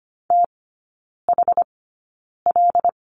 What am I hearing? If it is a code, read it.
Morse code, heard as a keyed tone, about 25 wpm, spaced as for 9 wpm, 700 Hz. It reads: THL
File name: Lesson 17e - Letter u - Three Characters 25wpm